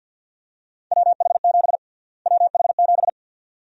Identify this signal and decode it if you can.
Morse code, heard as a keyed tone, about 47 wpm, 700 Hz. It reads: WH7 WH7
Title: USA Call Sign Prefixes - Repeated 2x - 50wpm